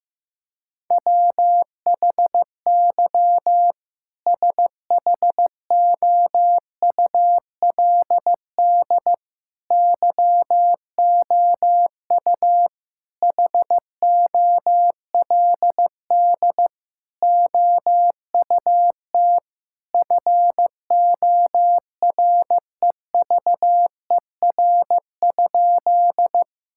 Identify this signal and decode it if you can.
Morse code, heard as a keyed tone, about 15 wpm, 700 Hz. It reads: WHY SHOULD YOU HOLD OUT FOREVER?
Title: Sentences from Top 1600 Words 15wpm